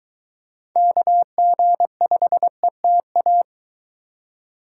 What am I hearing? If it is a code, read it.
Morse code, heard as a keyed tone, about 23 wpm, 700 Hz. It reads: KG5ETA